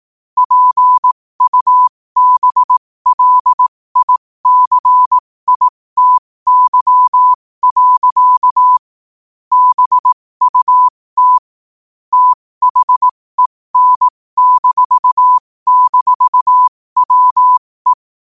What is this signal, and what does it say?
Morse code, heard as a keyed tone, about 18 wpm, spaced as for 15 wpm, 1000 Hz. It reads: PUBLICITY. BUT THEN--WE
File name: qotdfiles / 15wpm